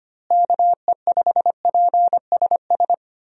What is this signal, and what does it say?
Morse code, heard as a keyed tone, about 25 wpm, 700 Hz. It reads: KE5PSS
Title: USA Call Signs - 6 Characters - Repeated 3x - 25wpm